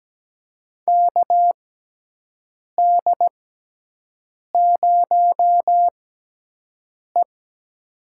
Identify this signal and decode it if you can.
Morse code, heard as a keyed tone, about 17 wpm, spaced as for 6 wpm, 700 Hz. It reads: KD0E
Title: USA Call Signs - 4 Characters - 6x Character Spacing - 17wpm